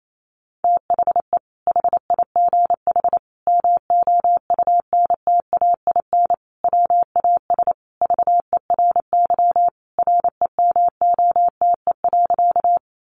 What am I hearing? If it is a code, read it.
Morse code, heard as a keyed tone, about 28 wpm, 700 Hz. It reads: THE HIGH MOUNTAIN WAS VERY REMOTE.
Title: Sentences from Top 1300 Words 28wpm